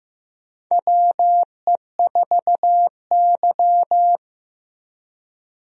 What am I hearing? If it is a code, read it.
Morse code, heard as a keyed tone, about 15 wpm, 700 Hz. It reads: WE4Y
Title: USA Call Signs - 4 Characters - 15wpm